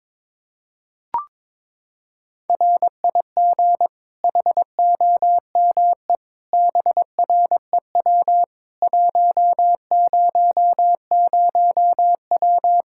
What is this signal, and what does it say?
Morse code, heard as a keyed tone, about 22 wpm, 700 Hz. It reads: RIG HOME BREW 100W
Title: Practice QSO - Exchange 2 - Info - 22wpm